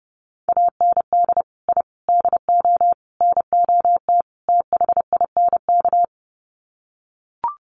Morse code, heard as a keyed tone, about 30 wpm, 700 Hz. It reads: AND I DO NOT THINK
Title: Sets of 5 Words 30wpm